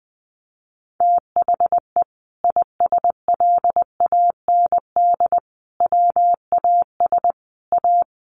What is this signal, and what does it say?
Morse code, heard as a keyed tone, about 20 wpm, 700 Hz. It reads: THE ISLAND WAS A
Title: Sentences from Top 500 Words 20wpm